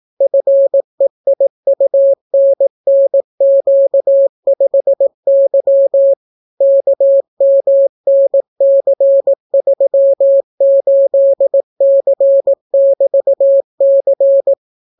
Morse code, heard as a keyed tone, about 18 wpm, 550 Hz. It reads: FEIUNNQ5Y KMNC38C<BT>C